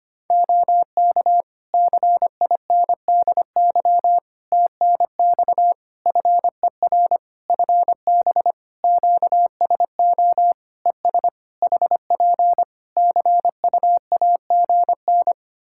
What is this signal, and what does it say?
Morse code, heard as a keyed tone, about 25 wpm, 700 Hz. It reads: OK CINDY TNX FER FB QSO ES HP CUAGN
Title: K1USN SST QSO Practice 25wpm